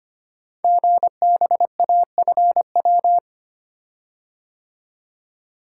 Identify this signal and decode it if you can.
Morse code, heard as a keyed tone, about 25 wpm, 700 Hz. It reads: GBAFW
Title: Code Groups - 5 Letters - 25wpm